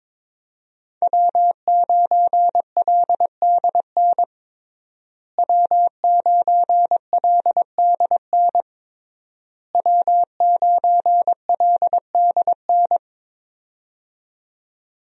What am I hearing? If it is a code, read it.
Morse code, heard as a keyed tone, about 23 wpm, 700 Hz. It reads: W9LDN W9LDN W9LDN
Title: USA Call Signs - 5 Characters - Repeated 3x - 22wpm